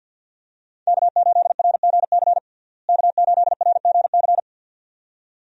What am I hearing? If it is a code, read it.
Morse code, heard as a keyed tone, about 52 wpm, 700 Hz. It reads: K8RGC K8RGC
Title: USA Call Signs - 5 Characters - Repeated 2x - 50wpm